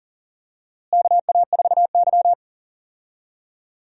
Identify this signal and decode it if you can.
Morse code, heard as a keyed tone, about 40 wpm, 700 Hz. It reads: KA4Y
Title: USA Call Signs - 4 Characters - Repeated 2x - 40wpm